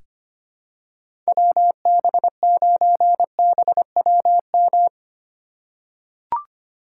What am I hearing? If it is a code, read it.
Morse code, heard as a keyed tone, about 25 wpm, 700 Hz. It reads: WB9BWM
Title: USA Call Signs - 6 Characters - 6x Character Spacing - 25wpm